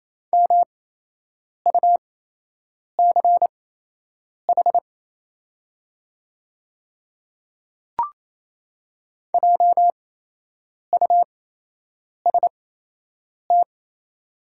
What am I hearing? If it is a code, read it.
Morse code, heard as a keyed tone, about 28 wpm, spaced as for 8 wpm, 690 Hz. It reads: MUCH JUST
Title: 8x Character Spacing - Top 500 Words - 28wpm